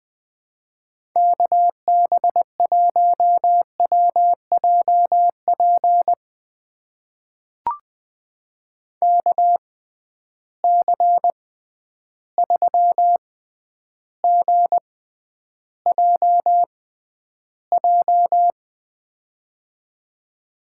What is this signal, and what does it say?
Morse code, heard as a keyed tone, about 21 wpm, spaced as for 10 wpm, 700 Hz. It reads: KB1WJP KC3GJJ
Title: USA Call Signs - 6 Characters - 6x Character Spacing - 20wpm